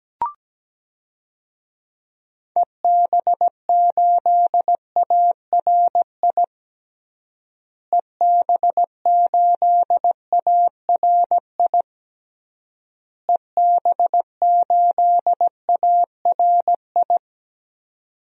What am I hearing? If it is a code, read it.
Morse code, heard as a keyed tone, about 17 wpm, 700 Hz. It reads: EB8ARI EB8ARI EB8ARI
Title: DXCC-100 Call Signs - 6 Characters - Repeated 3x - 17wpm